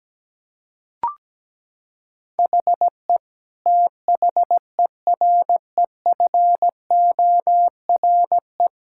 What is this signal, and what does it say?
Morse code, heard as a keyed tone, about 17 wpm, 700 Hz. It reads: HE THEREFORE
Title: Sentences from Top 1000 Words - Review Part 5 - 17wpm